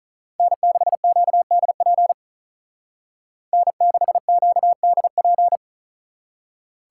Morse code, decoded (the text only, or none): N6QDP N6QDP